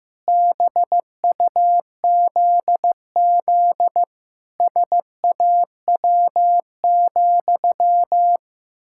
Morse code, heard as a keyed tone, about 15 wpm, 700 Hz. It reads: BUZZ SAW,
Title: Pangram Challenge v2 - 15wpm